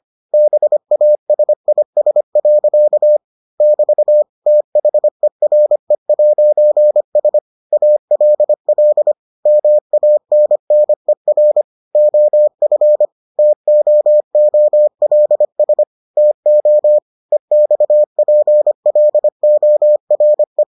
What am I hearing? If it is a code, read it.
Morse code, heard as a keyed tone, about 25 wpm, 600 Hz. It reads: BASIS. <BT> THERE'S ALL MANNER OF TOOLS TO EXPLORE